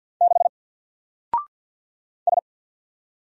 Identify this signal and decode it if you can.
Morse code, heard as a keyed tone, about 50 wpm, 700 Hz. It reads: 6 S
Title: Single Character (Letter:Number:Punctuation:Prosign) 50wpm